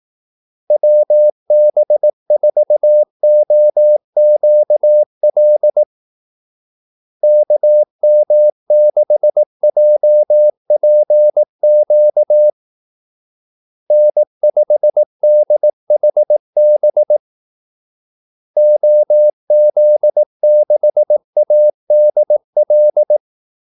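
Morse code, heard as a keyed tone, about 18 wpm, 600 Hz. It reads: WB4OQL KM6JPQ N5DHB OZ6ADL
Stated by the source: Morse call-sign practice at 18 wpm.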